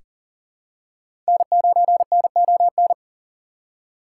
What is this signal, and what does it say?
Morse code, heard as a keyed tone, about 40 wpm, 700 Hz. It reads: N9NON
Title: USA Call Signs - 5 Characters - 4x Character Spacing - 40wpm